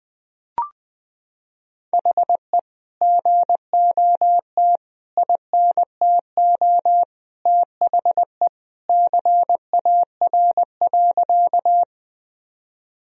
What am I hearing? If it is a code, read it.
Morse code, heard as a keyed tone, about 20 wpm, 700 Hz. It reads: HE GOT INTO THE CAR.
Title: Sentences from Top 1000 Words - Review Part 1 - 20wpm